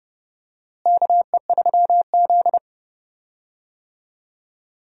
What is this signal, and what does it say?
Morse code, heard as a keyed tone, about 31 wpm, 700 Hz. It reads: KE3Z
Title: USA Call Signs - 4 Characters - Repeated 3x - 30wpm